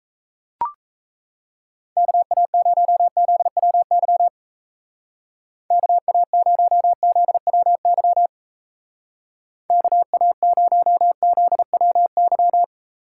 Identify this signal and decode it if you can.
Morse code, heard as a keyed tone, about 34 wpm, 700 Hz. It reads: KA0ZWY KA0ZWY KA0ZWY
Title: Speed Racing - USA Call Signs 28wpm